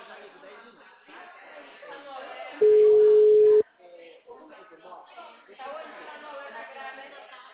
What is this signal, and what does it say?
Ringback tone (Europe, 425 Hz), with speech.